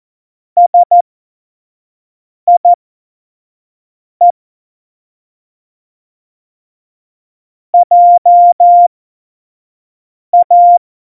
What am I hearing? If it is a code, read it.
Morse code, heard as a keyed tone, about 14 wpm, spaced as for 5 wpm, 700 Hz. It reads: SIE JA
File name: Morsekurs / Witze / Witz 24. September 2019, 25 F70